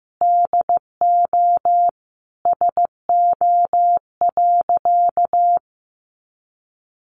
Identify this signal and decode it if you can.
Morse code, heard as a keyed tone, about 15 wpm, 700 Hz. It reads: DO SO.
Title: Sentences from Top 500 Words 15wpm